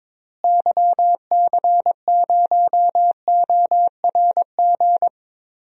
Morse code, heard as a keyed tone, about 22 wpm, 700 Hz. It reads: YC0ORG
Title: DXCC-100 Call Signs - 6 Characters - Repeated 2x - 22wpm